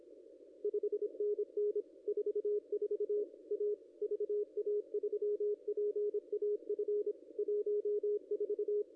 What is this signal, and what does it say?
Morse code, heard as a keyed tone, about 25 wpm, 410 Hz, with radio noise. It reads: 5NN 44 A VA3PAF 14
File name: contests / ss-cw-k5tr-op-n5rz -0100